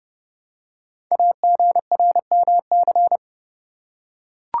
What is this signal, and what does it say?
Morse code, heard as a keyed tone, about 30 wpm, 700 Hz. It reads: AGRMC
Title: Code Groups - 5 Letters - 30wpm